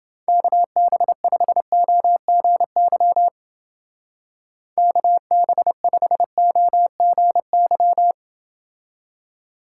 Morse code, heard as a keyed tone, about 29 wpm, 700 Hz. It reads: KB5OGY KB5OGY